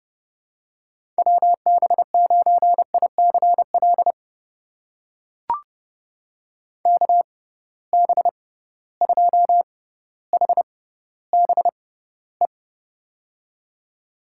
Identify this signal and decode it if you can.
Morse code, heard as a keyed tone, about 30 wpm, spaced as for 15 wpm, 700 Hz. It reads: WB9ICL KB2HBE
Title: USA Call Signs - 6 Characters - 6x Character Spacing - 30wpm